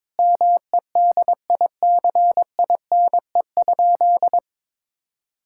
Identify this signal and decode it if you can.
Morse code, heard as a keyed tone, about 22 wpm, 700 Hz. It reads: MEDICINE?
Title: Sentences from Top 1000 Words - Review Part 6 - 22wpm